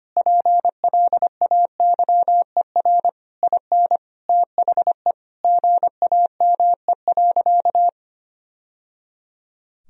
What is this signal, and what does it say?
Morse code, heard as a keyed tone, about 25 wpm, 700 Hz. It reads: PLAYER IN THE GAME.
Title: Sentences from Top 1000 Words - Review Part 4 - 25wpm